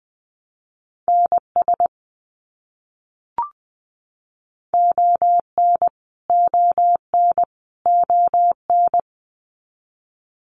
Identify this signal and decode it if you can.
Morse code, heard as a keyed tone, about 20 wpm, 700 Hz. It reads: NS ON ON ON
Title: Canadian Provinces and Territories - Repeated 3x - 20wpm